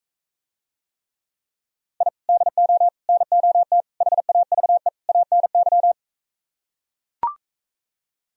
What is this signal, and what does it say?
Morse code, heard as a keyed tone, about 45 wpm, 700 Hz. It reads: I DO NOT HAVE ANY